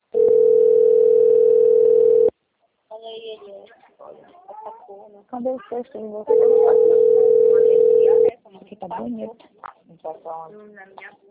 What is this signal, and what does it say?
Ringback tone (North America), 440 + 480 Hz, with speech.
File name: rg-1600-unknown- - - .22080